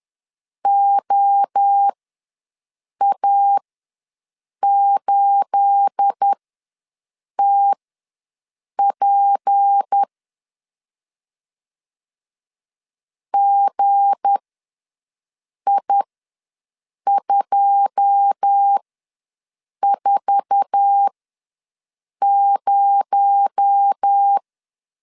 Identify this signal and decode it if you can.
Morse code, heard as a keyed tone, about 10 wpm, spaced as for 6 wpm, 800 Hz. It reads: OA8TP GI240
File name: LEZIONI / 9 286 ws 10wpm 4032